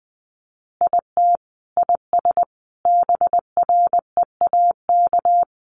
Morse code, heard as a keyed tone, about 20 wpm, 700 Hz. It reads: IT IS BREAK
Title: Easy Phrases 20wpm